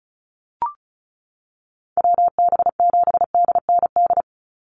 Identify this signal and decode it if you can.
Morse code, heard as a keyed tone, about 34 wpm, 700 Hz. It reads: WB7DND
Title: Call Signs Repeated 2x - 35wpm